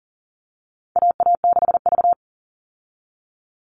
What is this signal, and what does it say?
Morse code, heard as a keyed tone, about 40 wpm, 700 Hz. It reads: AA6V
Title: All Cut Numbers - 4-Digit 40wpm